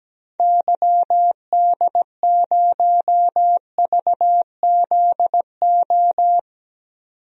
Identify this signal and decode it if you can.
Morse code, heard as a keyed tone, about 17 wpm, 700 Hz. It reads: YD0VZO
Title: DXCC-100 Call Signs - 6 Characters - 17wpm